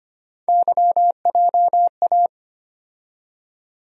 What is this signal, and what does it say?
Morse code, heard as a keyed tone, about 25 wpm, 700 Hz. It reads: YJA
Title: USA Call Sign Postfixes - 3 Letters 25wpm